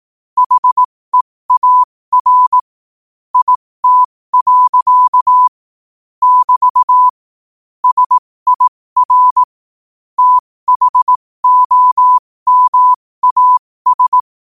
Morse code, heard as a keyed tone, about 18 wpm, spaced as for 15 wpm, 1000 Hz. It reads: HEAR IT. <BT> SIR THOMAS
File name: qotdfiles / 15wpm